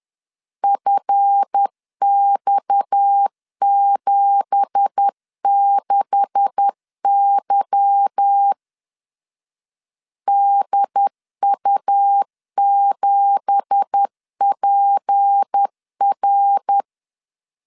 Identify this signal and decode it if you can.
Morse code, heard as a keyed tone, about 10 wpm, 800 Hz. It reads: FX76Y DU7PR
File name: LEZIONI / 9 2867 10wpm 2733